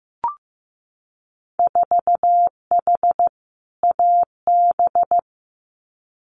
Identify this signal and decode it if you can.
Morse code, heard as a keyed tone, about 15 wpm, 700 Hz. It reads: 4H AB
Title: Winter Field Day Exchanges Encore 15wpm